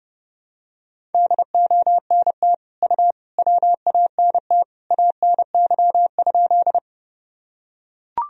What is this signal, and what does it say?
Morse code, heard as a keyed tone, about 32 wpm, 700 Hz. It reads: DONT U WANT ANY?
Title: Sentences from Top 300 Words and Abbreviations 30wpm